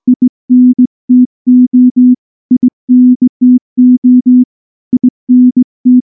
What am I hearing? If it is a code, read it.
Morse code, heard as a keyed tone, about 19 wpm, 250 Hz. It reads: INTO INTO INT
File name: morse / morse-training-20WPM-10